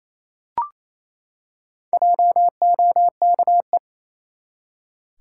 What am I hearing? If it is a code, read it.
Morse code, heard as a keyed tone, about 27 wpm, 700 Hz. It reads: JOKE
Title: Four Letter Words - 28wpm